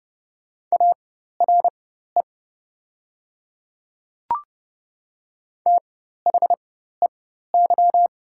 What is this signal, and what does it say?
Morse code, heard as a keyed tone, about 30 wpm, spaced as for 14 wpm, 690 Hz. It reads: ARE THEY
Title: 4x Character Spacing - Top 100 Words - 30wpm